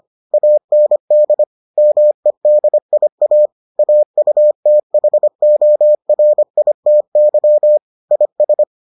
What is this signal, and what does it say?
Morse code, heard as a keyed tone, about 25 wpm, 600 Hz. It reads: AND MEDIA AUTHORITY IS